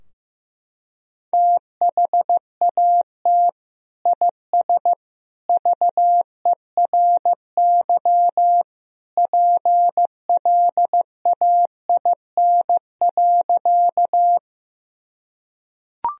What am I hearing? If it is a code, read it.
Morse code, heard as a keyed tone, about 15 wpm, 700 Hz. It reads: THAT IS VERY PLAIN.
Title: Sentences from Top 1000 Words - Review Part 2 - 15wpm